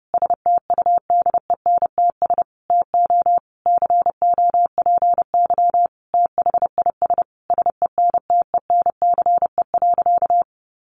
Morse code, heard as a keyed tone, about 30 wpm, 700 Hz. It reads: STUDENTS TO COPY THIS SENTENCE.
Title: Additional Easy Phrases 30wpm